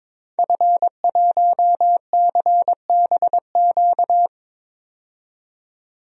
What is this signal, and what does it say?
Morse code, heard as a keyed tone, about 21 wpm, 700 Hz. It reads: F1CBQ
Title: DXCC-100 Call Signs - 5 Characters - 22wpm